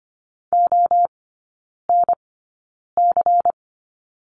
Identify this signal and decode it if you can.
Morse code, heard as a keyed tone, about 25 wpm, spaced as for 9 wpm, 700 Hz. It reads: ONC
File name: course / 25:10wpm / 093 Lesson 16e - Letter c - Three Characters 25wpm